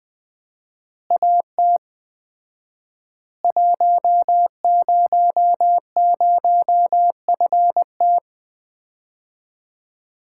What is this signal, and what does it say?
Morse code, heard as a keyed tone, about 20 wpm, 700 Hz. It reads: AT 100FT
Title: Common QSO Phrases - 4x Word-Spacing - 20wpm